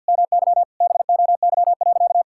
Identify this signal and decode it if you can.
Morse code, heard as a keyed tone, about 47 wpm, 700 Hz. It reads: MY BOY.
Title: Sentences from Top 1000 Words - Review Part 5 - 50wpm